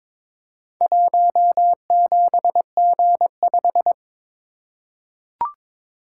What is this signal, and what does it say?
Morse code, heard as a keyed tone, about 22 wpm, 700 Hz. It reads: 17G5
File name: practice / 22wpm / Four Letter-Number Combinations - 22wpm